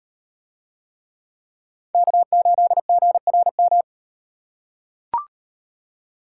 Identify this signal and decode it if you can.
Morse code, heard as a keyed tone, about 38 wpm, 700 Hz. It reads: K8GRM